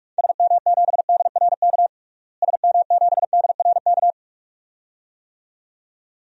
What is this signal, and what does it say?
Morse code, heard as a keyed tone, about 46 wpm, 700 Hz. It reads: SM7DRK SM7DRK
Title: DXCC-100 Call Signs - 6 Characters - Repeated 2x - 45wpm